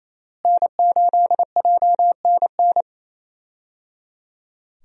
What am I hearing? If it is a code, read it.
Morse code, heard as a keyed tone, about 28 wpm, 700 Hz. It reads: N8JNN
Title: USA Call Signs - 5 Characters - Repeated 3x - 28wpm